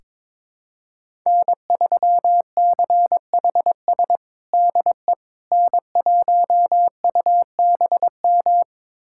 Morse code, heard as a keyed tone, about 22 wpm, 700 Hz. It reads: N3CHS DE N1UBM
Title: USA Callsign DE Callsign - 5 Characters - 22wpm